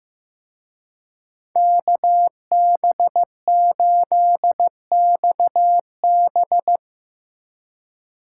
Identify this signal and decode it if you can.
Morse code, heard as a keyed tone, about 16 wpm, 700 Hz. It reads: KB8XB
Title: Speed Racing - USA Call Signs 15wpm